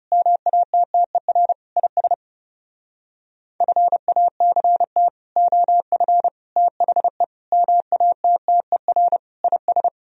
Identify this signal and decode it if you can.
Morse code, heard as a keyed tone, about 40 wpm, 700 Hz. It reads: MATTER IS FACT OF THE MATTER IS